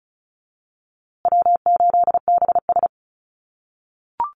Morse code, heard as a keyed tone, about 34 wpm, 700 Hz. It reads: W8BS
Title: USA Call Signs Repeated 2x 35wpm